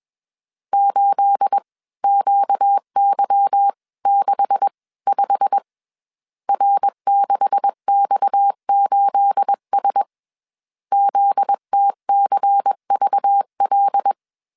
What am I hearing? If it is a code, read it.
Morse code, heard as a keyed tone, about 21 wpm, spaced as for 18 wpm, 800 Hz. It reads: 8QY65 R6X8S ZTCVL